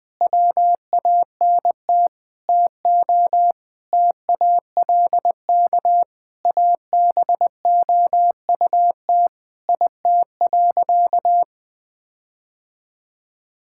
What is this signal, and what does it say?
Morse code, heard as a keyed tone, about 20 wpm, 700 Hz. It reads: WANT TO TALK ABOUT IT.
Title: Sentences with 5-Word Phrases - 20wpm